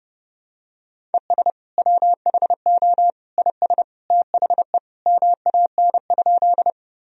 Morse code, heard as a keyed tone, about 32 wpm, 700 Hz. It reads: ES WHO IS THE MAN?
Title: Sentences from Top 200 Words and Abbreviations 30wpm